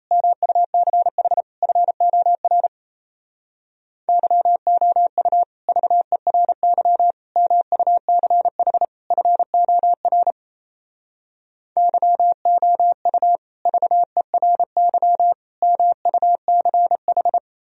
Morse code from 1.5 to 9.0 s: FOR YOU VERY MUCH

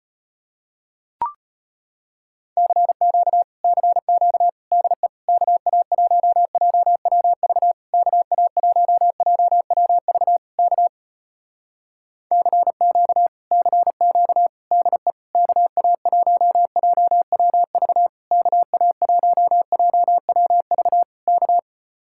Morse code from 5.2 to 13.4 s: KA1JWV KA1JWV K CQ

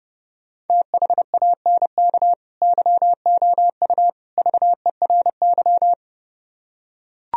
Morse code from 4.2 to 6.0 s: VERY

